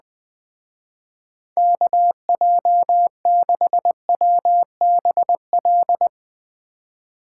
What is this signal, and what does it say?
Morse code, heard as a keyed tone, about 20 wpm, 700 Hz. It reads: KJ6WBL